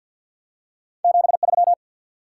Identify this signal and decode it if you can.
Morse code, heard as a keyed tone, about 48 wpm, 700 Hz. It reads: 73
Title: Top 180 QSO Elements 50wpm